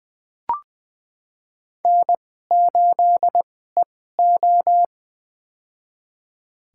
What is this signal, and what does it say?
Morse code, heard as a keyed tone, about 20 wpm, spaced as for 14 wpm, 700 Hz. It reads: N8EO
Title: USA Call Signs - 4 Characters - 2x Character Spacing - 20wpm